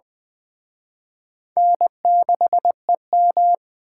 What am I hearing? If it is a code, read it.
Morse code, heard as a keyed tone, about 20 wpm, 700 Hz. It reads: N6EM